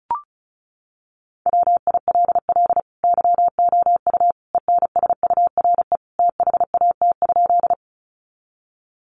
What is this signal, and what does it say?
Morse code, heard as a keyed tone, about 34 wpm, 700 Hz. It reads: WILL YOU ENSURE THAT?
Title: Sentences from Top 1100 Words 35wpm